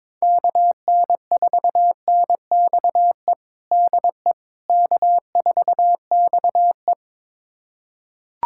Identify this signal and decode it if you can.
Morse code, heard as a keyed tone, about 22 wpm, 700 Hz. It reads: KN4NXE DE K4XE
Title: USA Callsign DE Callsign 22wpm